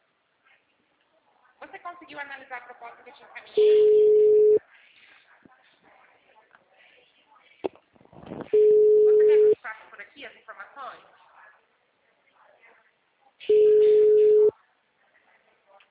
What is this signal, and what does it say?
Ringback tone (Europe, 425 Hz), with speech.